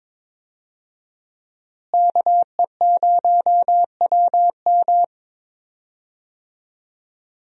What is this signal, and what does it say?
Morse code, heard as a keyed tone, about 22 wpm, 700 Hz. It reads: KE0WM
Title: USA Call Signs - 5 Characters - Repeated 3x - 22wpm